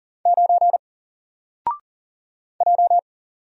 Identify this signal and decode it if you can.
Morse code, heard as a keyed tone, about 39 wpm, 700 Hz. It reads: 9 J
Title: Single Character (Letter:Number:Punctuation:Prosign) 40wpm